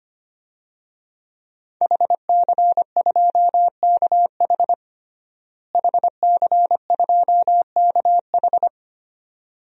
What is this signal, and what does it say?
Morse code, heard as a keyed tone, about 25 wpm, 700 Hz. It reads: HC2KH HC2KH
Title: DXCC-100 Call Signs - 5 Characters - Repeated 2x - 25wpm